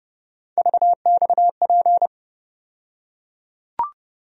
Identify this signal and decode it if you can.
Morse code, heard as a keyed tone, about 30 wpm, 700 Hz. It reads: VXP
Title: USA Call Sign Postfixes - 3 Letters 30wpm